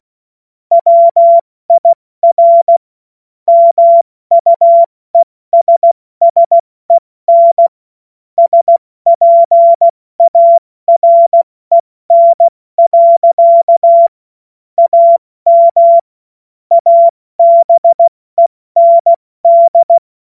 Morse code, heard as a keyed tone, about 16 wpm, 680 Hz. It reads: WIR MUESSEN SPAREN. AM ABEND
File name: Witz-08-Juli-2025-70-F80